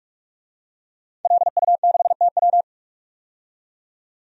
Morse code, heard as a keyed tone, about 42 wpm, 700 Hz. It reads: LU6TW
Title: DXCC-100 Call Signs - 5 Characters - 45wpm